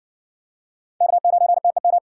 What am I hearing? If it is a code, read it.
Morse code, heard as a keyed tone, about 60 wpm, 700 Hz. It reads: K9TEM